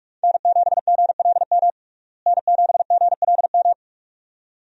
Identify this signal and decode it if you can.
Morse code, heard as a keyed tone, about 42 wpm, 700 Hz. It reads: N7GLM N7GLM
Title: USA Call Signs - 5 Characters - Repeated 2x - 45wpm